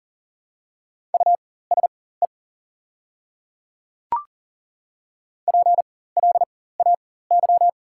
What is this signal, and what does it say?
Morse code, heard as a keyed tone, about 40 wpm, spaced as for 19 wpm, 700 Hz. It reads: USE PLAY
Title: 4x Character Spacing - Top 300 Words - 40wpm